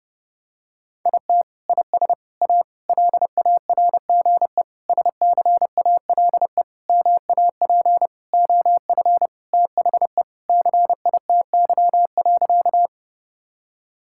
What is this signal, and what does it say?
Morse code, heard as a keyed tone, about 32 wpm, 700 Hz. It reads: IT IS A LARGE SCALE MAP OF THE CITY.
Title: Sentences from Top 1000 Words - Review Part 5 - 30wpm